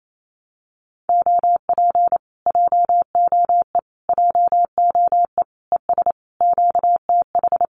Morse code, heard as a keyed tone, about 28 wpm, 700 Hz. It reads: OP JOE JOE ES QTH
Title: Common QSO Phrases Encore 28wpm